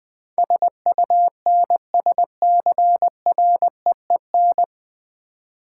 Morse code, heard as a keyed tone, about 20 wpm, 700 Hz. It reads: SUNSCREEN
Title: Easy Words with Hints - 20wpm